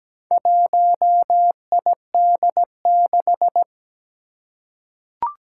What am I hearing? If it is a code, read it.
Morse code, heard as a keyed tone, about 17 wpm, 700 Hz. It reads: 1ID6
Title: Four Challenging Letter-Number Combinations 17wpm